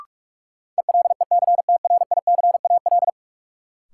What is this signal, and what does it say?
Morse code, heard as a keyed tone, about 46 wpm, 700 Hz. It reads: ELECTRICAL
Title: Sentences from Top 1000 Words - Review Part 6 - New Word Drill - 45wpm